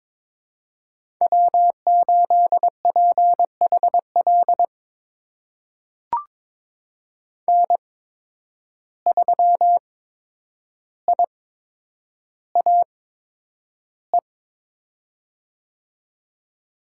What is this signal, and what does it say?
Morse code, heard as a keyed tone, about 22 wpm, spaced as for 9 wpm, 700 Hz. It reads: W8PHL N3IAE
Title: USA Call Signs - 5 Characters - 8x Character Spacing - 22wpm